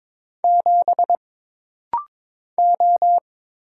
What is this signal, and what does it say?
Morse code, heard as a keyed tone, about 22 wpm, 700 Hz. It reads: 7 O
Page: This rapid-fire timing is geared to push your ICR (Instant Character Recognition) skills.